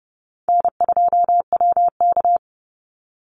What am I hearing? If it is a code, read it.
Morse code, heard as a keyed tone, about 30 wpm, 700 Hz. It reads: N2WK
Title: Easy Call Signs Repeated 2x - 30wpm